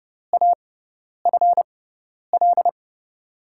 Morse code, heard as a keyed tone, about 31 wpm, spaced as for 10 wpm, 700 Hz. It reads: AFL
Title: USA Call Signs - 6 Characters - 6x Character Spacing - 30wpm